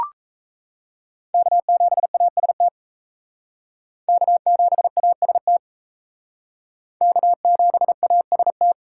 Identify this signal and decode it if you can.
Morse code, heard as a keyed tone, about 35 wpm, 700 Hz. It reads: K7AST K7AST K7AST